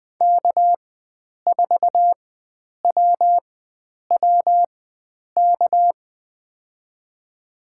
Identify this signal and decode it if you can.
Morse code, heard as a keyed tone, about 20 wpm, spaced as for 9 wpm, 700 Hz. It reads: K4WWK
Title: USA Call Signs - 5 Characters - 4x Character Spacing - 20wpm